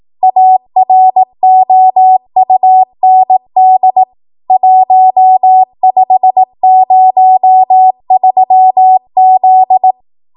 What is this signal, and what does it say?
Morse code, heard as a keyed tone, about 18 wpm, 750 Hz. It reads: AROUND 1503Z
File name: Archive / 18 WPM / 18WPM